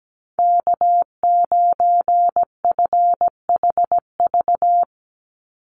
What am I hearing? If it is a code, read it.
Morse code, heard as a keyed tone, about 17 wpm, 700 Hz. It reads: K9FHV
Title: USA Call Signs 17wpm